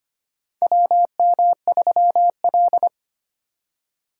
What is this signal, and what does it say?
Morse code, heard as a keyed tone, about 25 wpm, 700 Hz. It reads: WM3L
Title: USA Call Signs - 4 Characters - 4x Character Spacing - 25wpm